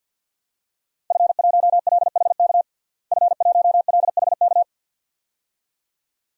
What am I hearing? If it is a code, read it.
Morse code, heard as a keyed tone, about 50 wpm, 700 Hz. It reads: F1LHK F1LHK
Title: DXCC-100 Call Signs - 5 Characters - Repeated 2x - 50wpm